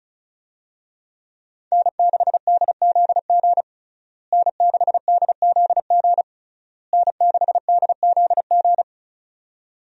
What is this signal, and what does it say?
Morse code, heard as a keyed tone, about 34 wpm, 700 Hz. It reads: N6DZG N6DZG N6DZG
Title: USA Call Signs - 5 Characters - Repeated 3x - 35wpm